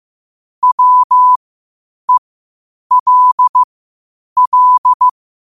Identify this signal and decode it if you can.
Morse code, heard as a keyed tone, about 15 wpm, spaced as for 8 wpm, 1000 Hz. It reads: WELL
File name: qotdfiles / 8wpm